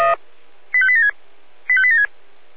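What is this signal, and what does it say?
Five-tone selective call (ZVEI), heard as quick runs of five tones, tones of about 70 ms. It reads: 85876 85876
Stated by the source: ZVEI five-tone call received on SDR.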